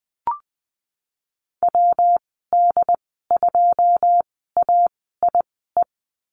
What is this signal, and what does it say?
Morse code, heard as a keyed tone, about 19 wpm, spaced as for 14 wpm, 700 Hz. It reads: WD2AIE
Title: 128 Lesson 21g - Number 6 - Call Signs 20wpm